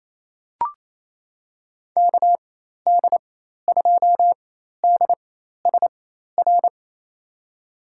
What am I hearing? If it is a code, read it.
Morse code, heard as a keyed tone, about 27 wpm, spaced as for 13 wpm, 700 Hz. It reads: KD2DSR
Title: USA Call Signs - 6 Characters - 4x Character Spacing - 28wpm